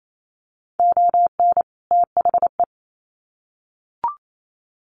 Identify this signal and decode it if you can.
Morse code, heard as a keyed tone, about 28 wpm, 700 Hz. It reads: ON THE
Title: Sets of 2 Words 28wpm